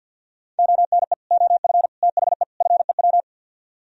Morse code, heard as a keyed tone, about 64 wpm, 700 Hz. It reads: ONE OF THE FEW